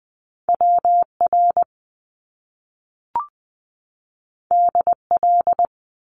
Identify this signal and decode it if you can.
Morse code, heard as a keyed tone, about 20 wpm, 700 Hz. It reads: WR DL
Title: Top Two Letter Combinations in Words 20wpm